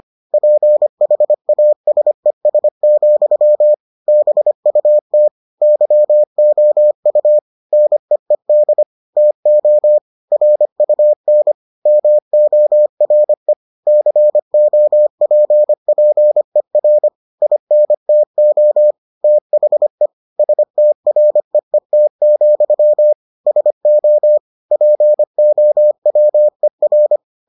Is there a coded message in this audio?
Yes: Morse code from 0.3 to 27.2 s, heard as a keyed tone, about 25 wpm, 600 Hz. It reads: PHASES, BUT YOU NEED TO RUN MORE COPPER INTO THE STREET, SO POWER